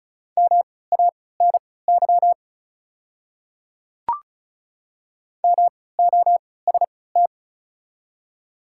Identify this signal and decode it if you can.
Morse code, heard as a keyed tone, about 34 wpm, spaced as for 20 wpm, 700 Hz. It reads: MANY MOST
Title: 3x Character Spacing - Top 100 Words - 35wpm